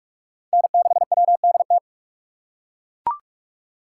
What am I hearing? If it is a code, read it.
Morse code, heard as a keyed tone, about 42 wpm, 700 Hz. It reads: N6WDT